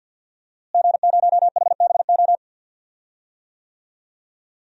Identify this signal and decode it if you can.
Morse code, heard as a keyed tone, about 52 wpm, 700 Hz. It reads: G0HBO
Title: DXCC-100 Call Signs - 5 Characters - Repeated 3x - 50wpm